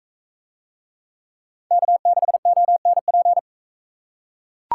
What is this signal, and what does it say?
Morse code, heard as a keyed tone, about 42 wpm, 700 Hz. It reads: K6ONP